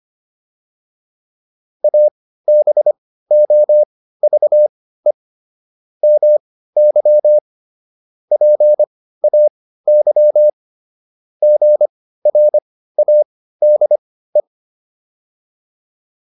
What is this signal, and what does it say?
Morse code, heard as a keyed tone, about 25 wpm, spaced as for 15 wpm, 600 Hz. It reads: ABOVE MY PAY GRADE